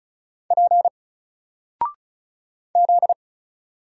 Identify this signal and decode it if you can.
Morse code, heard as a keyed tone, about 36 wpm, 700 Hz. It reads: P Z
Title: Single Character (Letter:Number:Punctuation:Prosign) 35wpm